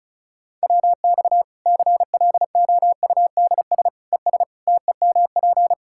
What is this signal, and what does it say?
Morse code, heard as a keyed tone, about 35 wpm, 700 Hz. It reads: WX CLOUDS ES TEMP
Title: Practice QSO - Exchange 2 - Info - 35wpm